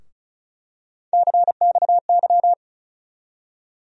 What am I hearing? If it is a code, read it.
Morse code, heard as a keyed tone, about 35 wpm, 700 Hz. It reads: CXY
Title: USA Call Sign Postfixes - 3 Letters 35wpm